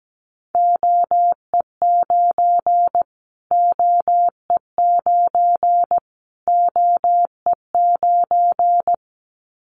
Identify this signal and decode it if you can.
Morse code, heard as a keyed tone, about 17 wpm, 700 Hz. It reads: OE9 OE9 OE9
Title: Callsign Prefixes for Top 25 DXCC - Repeated 3x - 17wpm